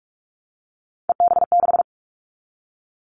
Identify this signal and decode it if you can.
Morse code, heard as a keyed tone, about 46 wpm, 700 Hz. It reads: EB6
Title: Callsign Prefixes for Top 100 DXCC - 45wpm